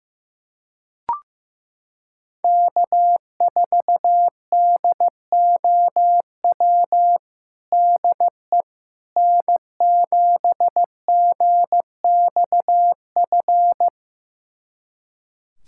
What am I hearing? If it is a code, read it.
Morse code, heard as a keyed tone, about 16 wpm, 700 Hz. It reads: K4DOW DE N7GXF
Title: USA Callsign DE Callsign - 5 Characters - 15wpm